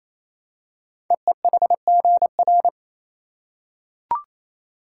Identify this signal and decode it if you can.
Morse code, heard as a keyed tone, about 27 wpm, 700 Hz. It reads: EEHGR
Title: Code Groups - 5 Letters - 28wpm